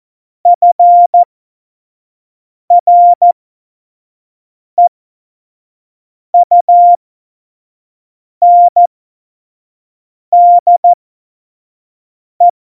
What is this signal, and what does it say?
Morse code, heard as a keyed tone, about 14 wpm, spaced as for 5 wpm, 700 Hz. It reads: FREUNDE
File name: Morsekurs / Witze / Witz 03. November 2020, 25 F70